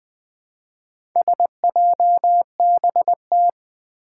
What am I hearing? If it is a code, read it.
Morse code, heard as a keyed tone, about 20 wpm, 700 Hz. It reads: SJBT
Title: Four Challenging Letter-Number Combinations 20wpm